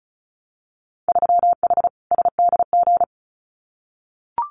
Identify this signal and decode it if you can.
Morse code, heard as a keyed tone, about 36 wpm, 700 Hz. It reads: 3H SDG
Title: Winter Field Day Exchanges Encore 35wpm